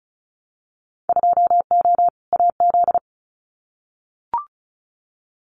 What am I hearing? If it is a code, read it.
Morse code, heard as a keyed tone, about 34 wpm, 690 Hz. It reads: 2O AZ
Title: Winter Field Day Exchanges 35wpm